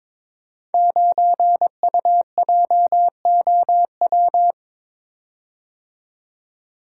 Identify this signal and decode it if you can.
Morse code, heard as a keyed tone, about 22 wpm, 700 Hz. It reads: 9UJOW
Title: Code Groups - 5 Letter:Numbers - 22wpm